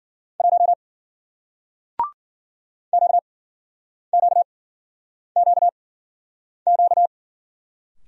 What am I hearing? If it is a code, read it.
Morse code, heard as a keyed tone, about 55 wpm, 700 Hz. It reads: 1 Q Q Q Q